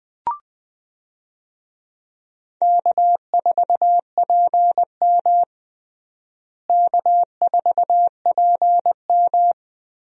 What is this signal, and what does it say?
Morse code, heard as a keyed tone, about 20 wpm, 700 Hz. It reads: K4PM K4PM
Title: USA Call Signs - 4 Characters - Repeated 3x - 20wpm